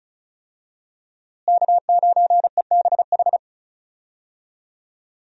Morse code, heard as a keyed tone, about 36 wpm, 700 Hz. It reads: K9EBH
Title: USA Call Signs - 5 Characters - Repeated 2x - 35wpm